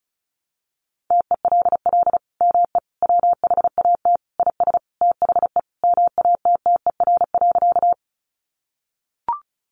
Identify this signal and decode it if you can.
Morse code, heard as a keyed tone, about 34 wpm, 700 Hz. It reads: TELL ME WHAT IS THE MATTER.
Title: Sentences from Top 700 Words 35wpm